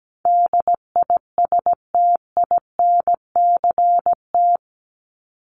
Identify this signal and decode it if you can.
Morse code, heard as a keyed tone, about 17 wpm, 700 Hz. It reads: DISTINCT
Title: Top 1501-1750 Words 17wpm